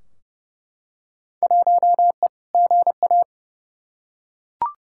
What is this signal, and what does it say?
Morse code, heard as a keyed tone, about 30 wpm, 700 Hz. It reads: 1E GA
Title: Field Day Exchanges - 30wpm